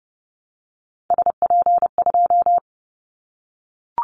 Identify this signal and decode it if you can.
Morse code, heard as a keyed tone, about 30 wpm, 700 Hz. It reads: SP2
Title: Callsign Prefixes for Top 25 DXCC - 30wpm